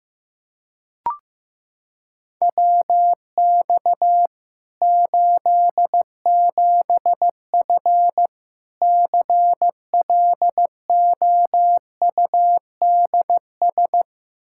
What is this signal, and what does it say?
Morse code, heard as a keyed tone, about 15 wpm, 700 Hz. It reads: WX 87F CLOUDS
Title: K1USN SST QSO Practice 15wpm